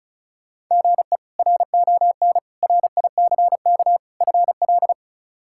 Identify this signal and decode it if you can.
Morse code, heard as a keyed tone, about 36 wpm, 700 Hz. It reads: GE RON RICK FL
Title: K1USN SST QSO Practice - 35wpm